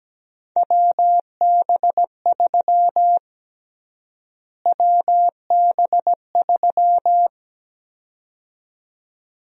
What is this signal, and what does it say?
Morse code, heard as a keyed tone, about 17 wpm, 700 Hz. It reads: WB3 WB3
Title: USA Call Sign Prefixes - Repeated 2x - 17wpm